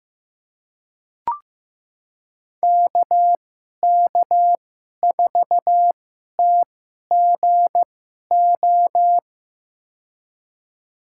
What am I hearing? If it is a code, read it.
Morse code, heard as a keyed tone, about 15 wpm, spaced as for 11 wpm, 700 Hz. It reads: KK4TGO
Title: USA Call Signs - 6 Characters - 2x Character Spacing - 15wpm